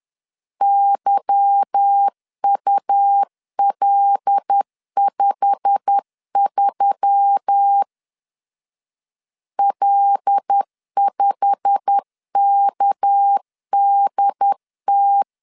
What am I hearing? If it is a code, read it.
Morse code, heard as a keyed tone, about 10 wpm, 800 Hz. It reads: YUL53 L5KDT
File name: LEZIONI / 9 2 10wpm 2736